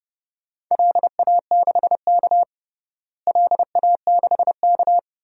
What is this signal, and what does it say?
Morse code, heard as a keyed tone, about 30 wpm, 700 Hz. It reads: LA6K LA6K
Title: DXCC-100 Call Signs - 4 Characters - Repeated 2x - 30wpm